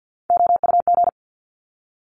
Morse code, heard as a keyed tone, about 50 wpm, 700 Hz. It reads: OUG
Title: Top Three Letter Combinations in Words 50wpm